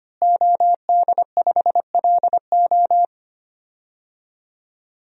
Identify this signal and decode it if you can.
Morse code, heard as a keyed tone, about 25 wpm, 700 Hz. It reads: OD5LO
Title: DXCC-100 Call Signs - 5 Characters - Repeated 3x - 25wpm